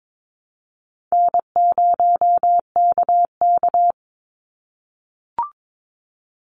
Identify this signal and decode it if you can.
Morse code, heard as a keyed tone, about 22 wpm, 700 Hz. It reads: N0KK
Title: Easy Call Signs - 22wpm